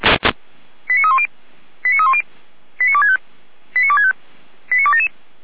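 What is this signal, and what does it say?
Five-tone selective call (ZVEI), heard as quick runs of five tones, tones of about 70 ms. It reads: 89310 89310 89265 89265 89277
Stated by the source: ZVEI five-tone selective call received via SDR, from the feuerwehr-orlamuende fire brigade audio server.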